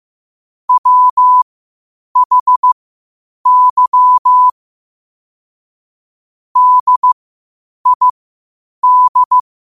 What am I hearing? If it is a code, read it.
Morse code, heard as a keyed tone, about 15 wpm, spaced as for 8 wpm, 1000 Hz. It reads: WHY DID
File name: 8wpm